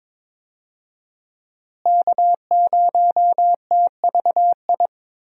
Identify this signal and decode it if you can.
Morse code, heard as a keyed tone, about 22 wpm, 700 Hz. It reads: K0TVI
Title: USA Call Signs - 5 Characters - Repeated 3x - 22wpm